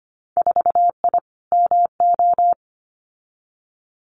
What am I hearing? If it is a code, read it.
Morse code, heard as a keyed tone, about 25 wpm, 690 Hz. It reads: 4I MO
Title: Winter Field Day Exchanges 25wpm